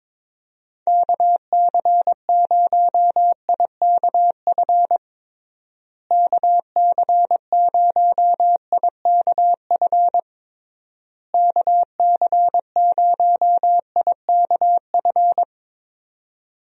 Morse code, heard as a keyed tone, about 22 wpm, 700 Hz. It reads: KC0IKF KC0IKF KC0IKF
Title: USA Call Signs - 6 Characters - Repeated 3x - 22wpm